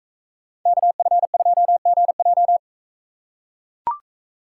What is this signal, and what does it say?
Morse code, heard as a keyed tone, about 40 wpm, 700 Hz. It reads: KF2GJ